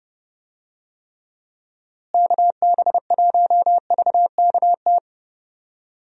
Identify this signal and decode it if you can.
Morse code, heard as a keyed tone, about 31 wpm, 700 Hz. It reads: KB1VKT
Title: USA Call Signs - 6 Characters - Repeated 3x - 30wpm